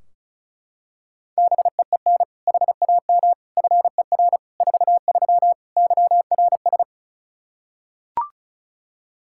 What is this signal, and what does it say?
Morse code, heard as a keyed tone, about 34 wpm, 700 Hz. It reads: BEEN HAM FER 43 YRS
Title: Practice QSO - Exchange 3 - More Info - 35wpm